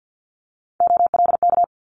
Morse code, heard as a keyed tone, about 50 wpm, 700 Hz. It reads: OLK
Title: Multi-letter Phonemes Quick Fire 50wpm